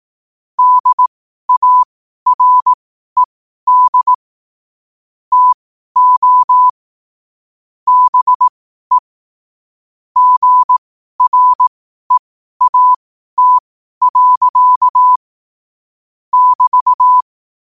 Morse code, heard as a keyed tone, about 18 wpm, spaced as for 12 wpm, 1000 Hz. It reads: DARED TO BE GREAT. <BT>
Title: 12 WPM morse code quotes for Fri, 01 Mar 2024